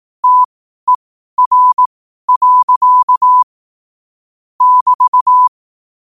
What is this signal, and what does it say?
Morse code, heard as a keyed tone, about 18 wpm, spaced as for 12 wpm, 1000 Hz. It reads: TER. <BT>
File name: qotdfiles / 12wpm